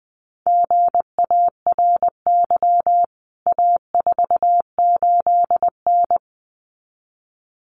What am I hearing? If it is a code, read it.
Morse code, heard as a keyed tone, about 20 wpm, 700 Hz. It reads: GARY A48N
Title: CWT Exchange 20wpm